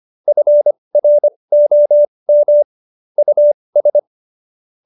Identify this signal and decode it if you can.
Morse code, heard as a keyed tone, about 25 wpm, spaced as for 20 wpm, 590 Hz. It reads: FROM US